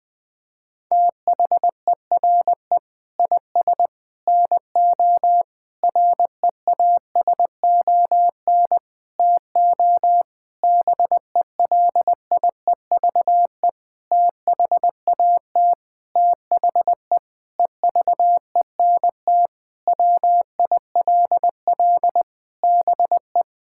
Morse code, heard as a keyed tone, about 20 wpm, 700 Hz. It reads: THERE IS NO REASON TO BELIEVE THAT THE EVENT WILL BE
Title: Sentences with 8-Word Phrases - 20wpm